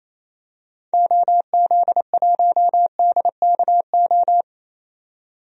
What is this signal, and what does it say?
Morse code, heard as a keyed tone, about 28 wpm, 700 Hz. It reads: OZ1DKO
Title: DXCC-100 Call Signs - 6 Characters - 28wpm